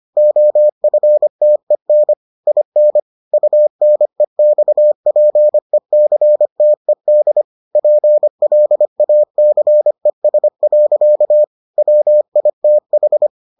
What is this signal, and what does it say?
Morse code, heard as a keyed tone, about 25 wpm, 600 Hz. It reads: OFTEN IN UNEXPECTED PLACES. WITH